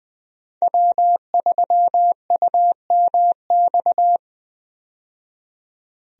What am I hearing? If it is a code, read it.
Morse code, heard as a keyed tone, about 20 wpm, 700 Hz. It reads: W3UMX
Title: USA Call Signs - 5 Characters - 20wpm